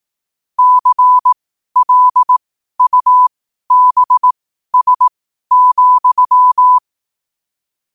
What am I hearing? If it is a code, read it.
Morse code, heard as a keyed tone, about 18 wpm, spaced as for 12 wpm, 1000 Hz. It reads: CLUBS,
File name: qotdfiles / 12wpm